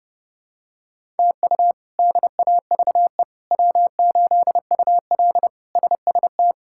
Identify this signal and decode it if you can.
Morse code, heard as a keyed tone, about 32 wpm, 700 Hz. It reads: TU DAVE W8UL SST
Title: K1USN SST QSO Practice - 30wpm